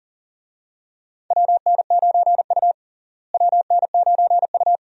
Morse code, heard as a keyed tone, about 40 wpm, 700 Hz. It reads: WN9U WN9U
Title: USA Call Signs - 4 Characters - Repeated 2x - 40wpm